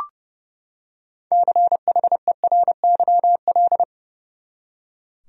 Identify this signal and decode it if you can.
Morse code, heard as a keyed tone, about 30 wpm, 700 Hz. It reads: CHERYL
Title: USA Common Names - 30wpm